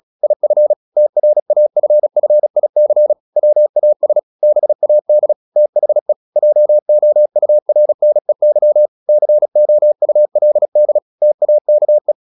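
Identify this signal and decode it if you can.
Morse code, heard as a keyed tone, about 36 wpm, 600 Hz. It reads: IF TRAFFIC WAS BAD THE JOURNEY COULD TAKE
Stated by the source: Morse practice at 35 wpm from CW med Gnister, sending Ken Follett's novel Never in English.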